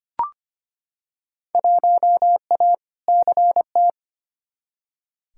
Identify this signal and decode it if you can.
Morse code, heard as a keyed tone, about 25 wpm, 700 Hz. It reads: 1A CT
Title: Field Day Exchanges Encore - 25wpm